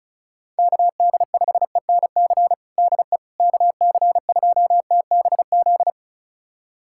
Morse code, heard as a keyed tone, about 34 wpm, 700 Hz. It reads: KD5ENC DE KC2TBZ
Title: USA Callsign DE Callsign - 6 Characters - 35wpm